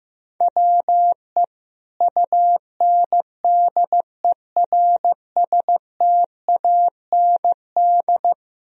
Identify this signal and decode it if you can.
Morse code, heard as a keyed tone, about 15 wpm, 700 Hz. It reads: WE UNDERSTAND
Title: Sentences from Top 1000 Words - Review Part 2 - 15wpm